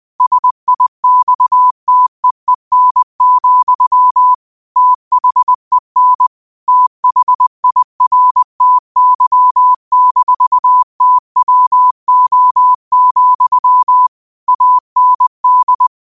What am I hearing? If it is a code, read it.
Morse code, heard as a keyed tone, about 20 wpm, 1000 Hz. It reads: SIXTEEN, THEN THIRTY-TWO, AND
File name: qotdfiles / 20wpm